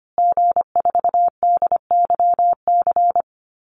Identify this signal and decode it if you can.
Morse code, heard as a keyed tone, about 25 wpm, 700 Hz. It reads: G4DYC
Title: Call Signs Repeated 3x - 25wpm